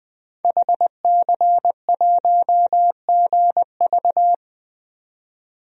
Morse code, heard as a keyed tone, about 20 wpm, 700 Hz. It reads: HC1GV
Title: DXCC-100 Call Signs - 5 Characters - 20wpm